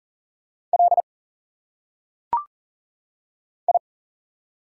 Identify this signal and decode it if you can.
Morse code, heard as a keyed tone, about 40 wpm, 700 Hz. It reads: L I
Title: Single Letters - Phonetic Pronunciation 40wpm